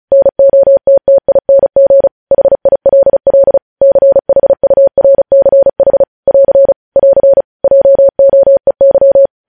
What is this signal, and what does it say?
Morse code, heard as a keyed tone, about 36 wpm, 550 Hz. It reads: NOTTING HILL CHURCH <AR> <AR> JOEY
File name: podcast / News headlines-1x-35x35